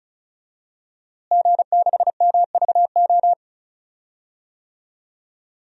Morse code, heard as a keyed tone, about 34 wpm, 700 Hz. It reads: G6MVO
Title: DXCC-100 Call Signs - 5 Characters - Repeated 2x - 35wpm